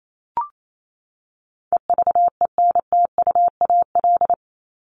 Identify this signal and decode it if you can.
Morse code, heard as a keyed tone, about 28 wpm, 700 Hz. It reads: EVENTUAL
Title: LY Suffix - Top 100 Words with Base Words 28wpm